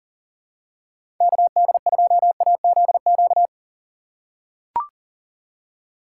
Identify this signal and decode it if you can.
Morse code, heard as a keyed tone, about 40 wpm, 700 Hz. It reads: KD2AZQ
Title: USA Call Signs - 6 Characters - 3x Character Spacing - 40wpm